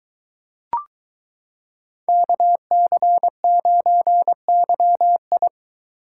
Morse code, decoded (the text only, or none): KC9YI